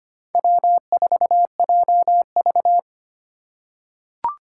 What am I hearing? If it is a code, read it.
Morse code, heard as a keyed tone, about 25 wpm, 700 Hz. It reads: W4JV
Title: USA Call Signs - 4 Characters - 4x Character Spacing - 25wpm